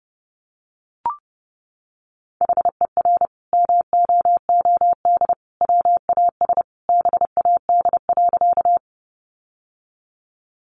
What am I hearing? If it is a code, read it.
Morse code, heard as a keyed tone, about 28 wpm, 700 Hz. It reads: HER MOOD WAS BAD.
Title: Sentences from Top 1100 Words 30wpm